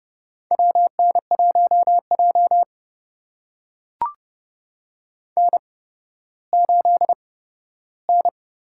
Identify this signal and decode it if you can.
Morse code, heard as a keyed tone, about 29 wpm, spaced as for 14 wpm, 700 Hz. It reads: WN1J N8N
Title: USA Call Signs - 8x Character Spacing - 30wpm